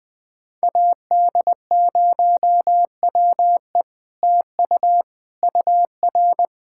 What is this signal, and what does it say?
Morse code, heard as a keyed tone, about 20 wpm, 700 Hz. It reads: AD0WE TU UR
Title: Speed-Racing Common QSO Phrases - 20wpm